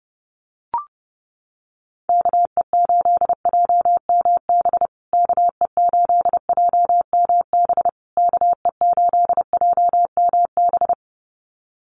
Morse code, heard as a keyed tone, about 30 wpm, 700 Hz. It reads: KE8JMB KE8JMB KE8JMB
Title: USA Call Signs Repeated 3x 30wpm